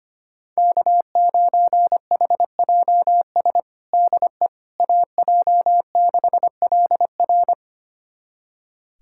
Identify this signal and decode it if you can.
Morse code, heard as a keyed tone, about 25 wpm, 700 Hz. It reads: K9HJS DE AJ6LR
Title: USA Callsign DE Callsign - 5 Characters - 25wpm